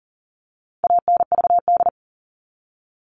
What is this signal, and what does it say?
Morse code, heard as a keyed tone, about 37 wpm, 700 Hz. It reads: ANVD
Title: All Cut Numbers - 4-Digit 40wpm